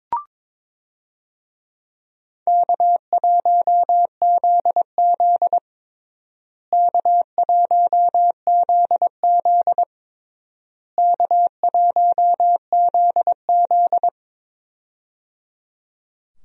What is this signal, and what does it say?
Morse code, heard as a keyed tone, about 22 wpm, 700 Hz. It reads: K1ZZ K1ZZ K1ZZ
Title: USA Call Signs - 4 Characters - Repeated 3x - 22wpm